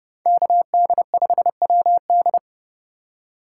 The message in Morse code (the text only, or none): KD5WD